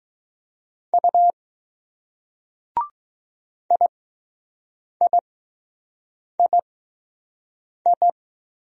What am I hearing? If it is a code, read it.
Morse code, heard as a keyed tone, about 20 wpm, spaced as for 6 wpm, 700 Hz. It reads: U IIII